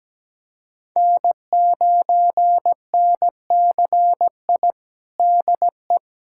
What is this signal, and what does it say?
Morse code, heard as a keyed tone, about 17 wpm, 700 Hz. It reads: N9NCI DE
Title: USA Callsign DE Callsign - 5 Characters - 17wpm